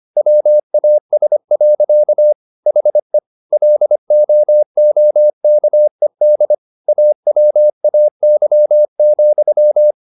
Morse code, heard as a keyed tone, about 25 wpm, 600 Hz. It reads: WAS. HE LOOKED AWAY,